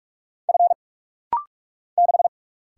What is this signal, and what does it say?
Morse code, heard as a keyed tone, about 44 wpm, 690 Hz. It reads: F 6
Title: Single Letter-Number - Rapid-Fire 45wpm